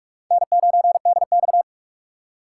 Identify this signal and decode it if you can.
Morse code, heard as a keyed tone, about 44 wpm, 700 Hz. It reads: N9DX